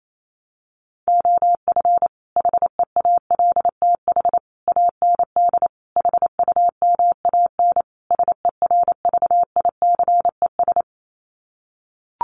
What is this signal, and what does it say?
Morse code, heard as a keyed tone, about 28 wpm, 700 Hz. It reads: OF HEALTH AND HUMAN SERVICES
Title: Sets of 5 Words 28wpm